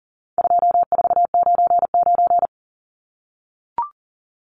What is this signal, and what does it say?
Morse code, heard as a keyed tone, about 40 wpm, 700 Hz. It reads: 2499
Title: Numbers - 4-Digit 40wpm